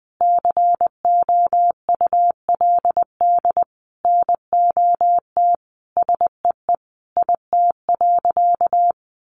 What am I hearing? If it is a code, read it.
Morse code, heard as a keyed tone, about 20 wpm, 700 Hz. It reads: COULD NOT SEE IT.
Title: Sentences from Top 1000 Words 20wpm